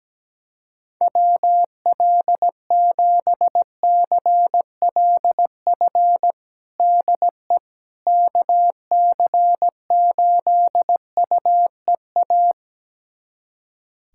Morse code, heard as a keyed tone, about 17 wpm, 700 Hz. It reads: WL7CLF DE KC8UEA
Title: USA Callsign DE Callsign - 6 Characters - 17wpm